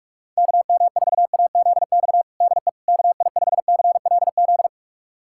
Morse code, heard as a keyed tone, about 41 wpm, 700 Hz. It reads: KM4AZX DE KI5CLZ
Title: USA Callsign DE Callsign - 6 Characters - 45wpm